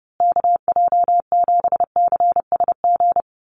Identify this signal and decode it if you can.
Morse code, heard as a keyed tone, about 30 wpm, 700 Hz. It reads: KJ7CSG
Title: USA Call Signs Repeated 2x 30wpm